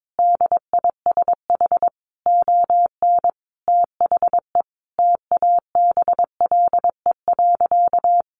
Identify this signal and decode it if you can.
Morse code, heard as a keyed tone, about 22 wpm, 700 Hz. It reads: DISH ON THE TABLE.
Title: Sentences from Top 1300 Words 22wpm